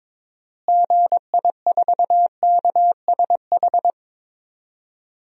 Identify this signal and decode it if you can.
Morse code, heard as a keyed tone, about 22 wpm, 700 Hz. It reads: GI4KSH
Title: DXCC-100 Call Signs - 6 Characters - Repeated 2x - 22wpm